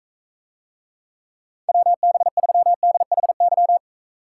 Morse code, heard as a keyed tone, about 44 wpm, 700 Hz. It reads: WB3DHY